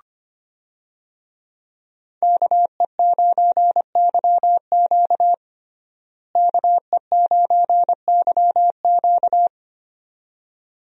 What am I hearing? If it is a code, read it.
Morse code, heard as a keyed tone, about 25 wpm, 700 Hz. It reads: KE9YQ KE9YQ
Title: USA Call Signs - 5 Characters - Repeated 2x - 25wpm